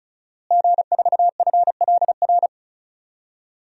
Morse code, heard as a keyed tone, about 34 wpm, 700 Hz. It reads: G4FLR
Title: DXCC-100 Call Signs - 5 Characters - Repeated 3x - 35wpm